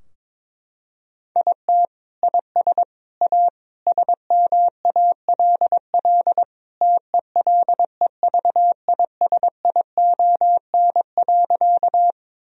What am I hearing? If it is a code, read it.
Morse code, heard as a keyed tone, about 22 wpm, 700 Hz. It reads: IT IS A SMALL TELEVISION.
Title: Sentences from Top 1000 Words - Review Part 4 - 22wpm